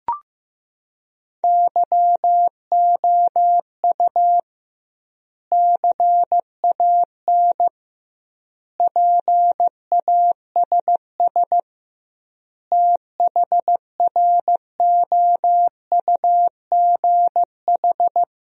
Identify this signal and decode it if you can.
Morse code, heard as a keyed tone, about 15 wpm, 700 Hz. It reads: YOU CAN PASS THROUGH
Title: 2x Word-Spacing - Sentences from Top 400 Words - 15wpm